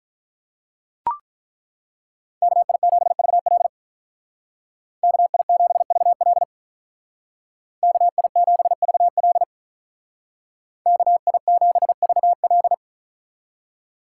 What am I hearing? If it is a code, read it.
Morse code, heard as a keyed tone, about 43 wpm, 700 Hz. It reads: KI7VL KI7VL KI7VL KI7VL
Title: Speed Racing - USA 2x2 Call Signs 35wpm